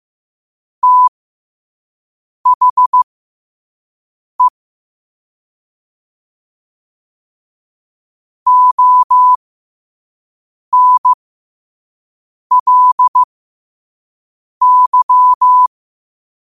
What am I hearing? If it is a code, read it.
Morse code, heard as a keyed tone, about 15 wpm, spaced as for 5 wpm, 1000 Hz. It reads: THE ONLY
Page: New quotes every day in morse code at 5 Words per minute.